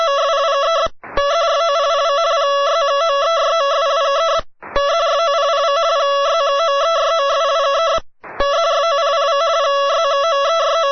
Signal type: two-tone teleprinter signal